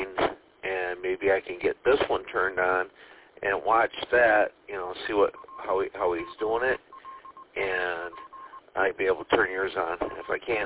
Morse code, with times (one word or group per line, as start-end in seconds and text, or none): none